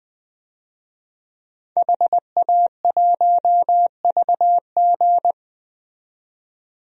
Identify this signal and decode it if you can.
Morse code, heard as a keyed tone, about 20 wpm, 700 Hz. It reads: HA1VG
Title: DXCC-100 Call Signs - 5 Characters - 20wpm